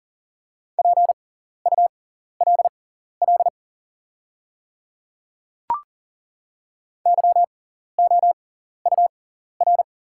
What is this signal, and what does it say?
Morse code, heard as a keyed tone, about 41 wpm, spaced as for 14 wpm, 690 Hz. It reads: PULL YOUR
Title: 6x Character Spacing - Top 500 Words - 40wpm